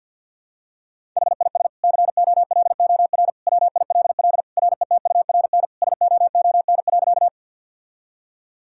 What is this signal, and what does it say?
Morse code, heard as a keyed tone, about 50 wpm, 700 Hz. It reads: HIS COLOR WILL RETURN SOON.
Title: Sentences from Top 1000 Words - Review Part 2 - 50wpm